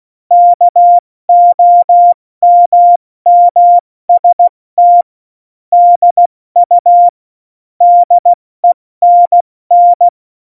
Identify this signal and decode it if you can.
Morse code, heard as a keyed tone, about 16 wpm, spaced as for 14 wpm, 700 Hz. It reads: KOMMST DU DENN
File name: Morsekurs / Witze / Witz 10. Oktober 17, Tempo 70